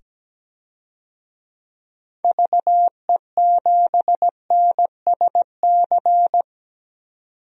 Morse code, heard as a keyed tone, about 17 wpm, 700 Hz. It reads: VE7NSC
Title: DXCC-100 Call Signs - 6 Characters - 17wpm